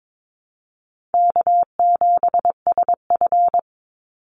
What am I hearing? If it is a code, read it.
Morse code, heard as a keyed tone, about 22 wpm, 690 Hz. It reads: K7SF
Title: Easy Call Signs Repeated 2x - 22wpm